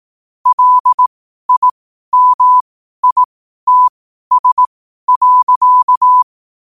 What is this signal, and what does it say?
Morse code, heard as a keyed tone, about 18 wpm, spaced as for 12 wpm, 1000 Hz. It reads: LIMITS.
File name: qotdfiles / 12wpm